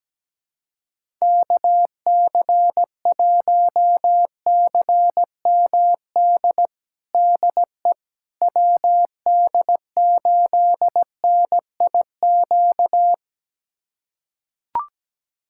Morse code, heard as a keyed tone, about 17 wpm, 700 Hz. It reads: KC1CMD DE WD8NIQ
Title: USA Callsign DE Callsign 17wpm